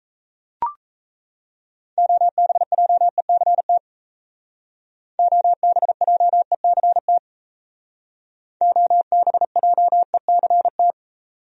Morse code, heard as a keyed tone, about 36 wpm, 700 Hz. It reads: OBJECT OBJECT OBJECT